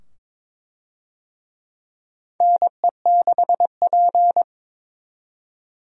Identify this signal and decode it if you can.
Morse code, heard as a keyed tone, about 22 wpm, 700 Hz. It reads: NE6P
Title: USA Call Signs - 4 Characters - 22wpm